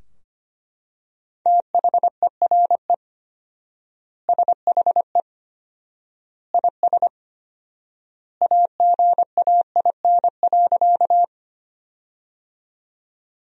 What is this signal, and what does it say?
Morse code, heard as a keyed tone, about 25 wpm, 700 Hz. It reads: THERE SHE IS AGAIN.
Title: 4x Word-Spacing - Sentences from Top 200 Words - 25wpm